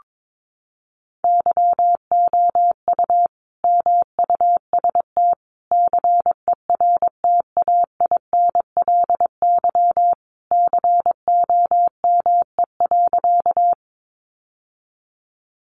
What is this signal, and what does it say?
Morse code, heard as a keyed tone, about 22 wpm, 700 Hz. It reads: YOU MUST CERTAINLY COME.
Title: Sentences from Top 900 Words 22wpm